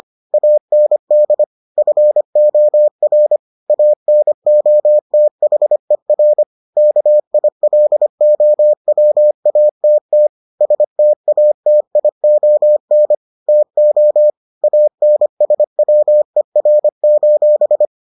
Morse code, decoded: AND FOR ANOTHER KILOWATT STATION TO ANSWER:

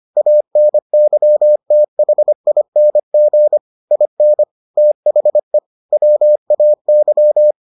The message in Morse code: ANYTHING IN THE WAY

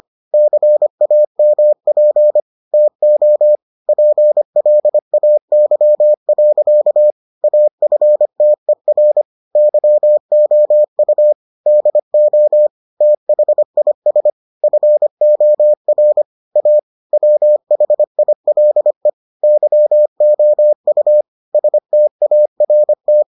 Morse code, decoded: CAMP TO PLAY. AFTER YOU DO THIS FOR A WHILE YOU START

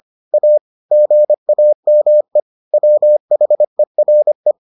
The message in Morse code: A GAME WHERE

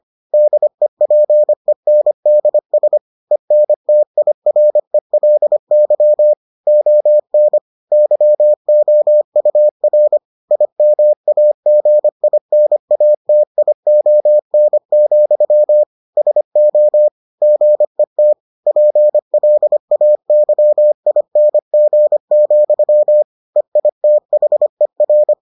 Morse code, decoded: DEPENDS ENTIRELY ON YOUR IMAGINATION, SO GET PLAYING, EITHER

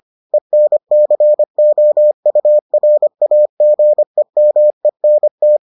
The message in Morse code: ENCOURAGEMENT

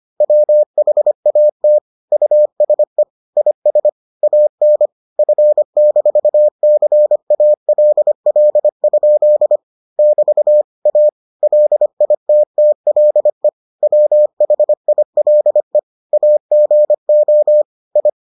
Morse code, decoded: WHAT USE IS AN F-CALL? <BT> A LITTLE WHILE AGO I

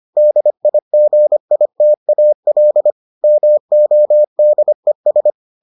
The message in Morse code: DIGITAL MODES

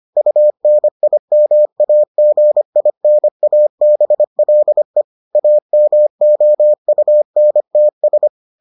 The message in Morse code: UNIMAGINABLE AMOUNTS

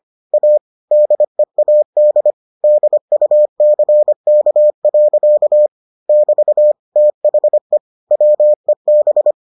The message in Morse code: A DEAD DUCK. <BT> THE WEB

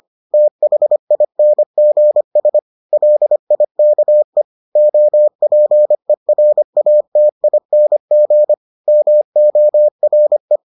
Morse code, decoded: THINGS LIKE OPERATING MORE

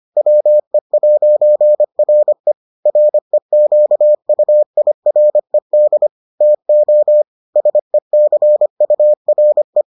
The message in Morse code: WE'RE REQUIRED TO SECURE

